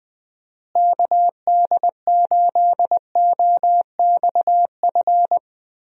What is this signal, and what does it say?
Morse code, heard as a keyed tone, about 20 wpm, 700 Hz. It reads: KD8OXF
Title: USA Call Signs - 6 Characters - Repeated 3x - 20wpm